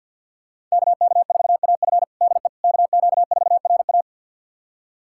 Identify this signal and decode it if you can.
Morse code, heard as a keyed tone, about 46 wpm, 690 Hz. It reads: KK4AF DE KQ4RA
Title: USA Callsign DE Callsign - 5 Characters - 50wpm